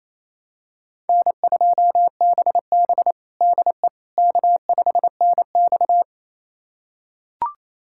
Morse code, heard as a keyed tone, about 28 wpm, 700 Hz. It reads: N2BB DE K5NX
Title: USA Callsign DE Callsign - 4 Characters - 28wpm